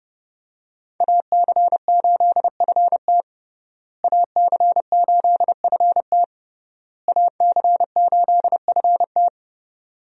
Morse code, heard as a keyed tone, about 32 wpm, 700 Hz. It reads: AC8FT AC8FT AC8FT
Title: USA Call Signs - 5 Characters - Repeated 3x - 30wpm